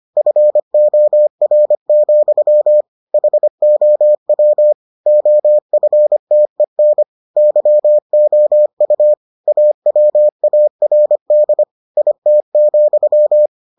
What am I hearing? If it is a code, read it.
Morse code, heard as a keyed tone, about 25 wpm, 600 Hz. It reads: FOR, HOW OFTEN YOU AWARD IT,